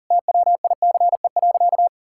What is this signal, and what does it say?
Morse code, heard as a keyed tone, about 40 wpm, 700 Hz. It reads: TWICE.
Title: Sentences from Top 1600 Words 40wpm